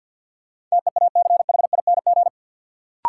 Morse code, heard as a keyed tone, about 52 wpm, 700 Hz. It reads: TEACHING
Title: Sentences from Top 1100 Words - New Word Drill - 50wpm